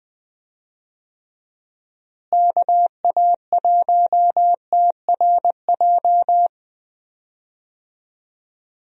Morse code, heard as a keyed tone, about 20 wpm, 700 Hz. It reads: KA1TRJ
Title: USA Call Signs - 6 Characters - Repeated 2x - 20wpm